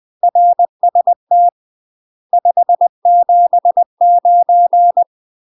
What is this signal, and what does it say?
Morse code, heard as a keyed tone, about 20 wpm, 700 Hz. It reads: RST 579
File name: UK MASTER QSOs TRAINING / 20WPM / UK MASTER QSOs 0008